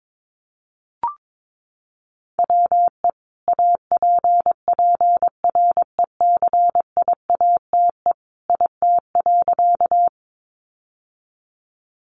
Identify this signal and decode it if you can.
Morse code, heard as a keyed tone, about 22 wpm, 700 Hz. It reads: WE APPRECIATE IT.
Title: Sentences from Top 1300 Words 22wpm